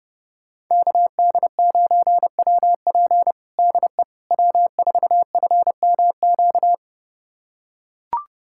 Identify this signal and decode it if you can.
Morse code, heard as a keyed tone, about 32 wpm, 700 Hz. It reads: KD9WP DE W4FMQ
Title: USA Callsign DE Callsign - 5 Characters - 30wpm